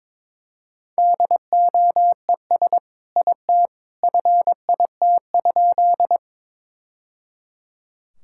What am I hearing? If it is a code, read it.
Morse code, heard as a keyed tone, about 22 wpm, 700 Hz. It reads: DOES IT FIT?
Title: Sentences from Top 1000 Words - Review Part 5 - 22wpm